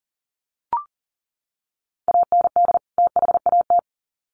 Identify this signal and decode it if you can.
Morse code, heard as a keyed tone, about 40 wpm, 700 Hz. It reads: AND THAT
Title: Sets of 2 Words 40wpm